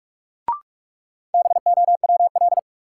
Morse code, heard as a keyed tone, about 42 wpm, 700 Hz. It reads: BOWL
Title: Sentences from Top 1200 Words - New Word Drill - 45wpm